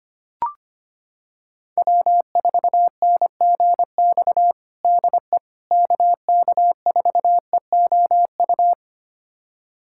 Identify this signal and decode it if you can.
Morse code, heard as a keyed tone, about 25 wpm, 700 Hz. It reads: W4NGX DE KK4EOU
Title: USA Callsign DE Callsign 25wpm